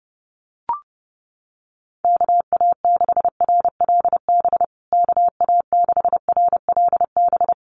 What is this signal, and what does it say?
Morse code, heard as a keyed tone, about 30 wpm, 700 Hz. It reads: KA6RLB KA6RLB
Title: USA Call Signs Repeated 3x 30wpm